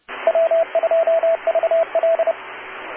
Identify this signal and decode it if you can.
Morse code, heard as a keyed tone, about 30 wpm, 660 Hz, with radio noise. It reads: W2VL